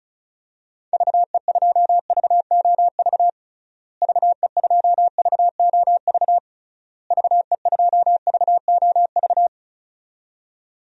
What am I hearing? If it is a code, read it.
Morse code, heard as a keyed tone, about 36 wpm, 700 Hz. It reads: VE2VOV VE2VOV VE2VOV
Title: DXCC-100 Call Signs - 6 Characters - Repeated 3x - 35wpm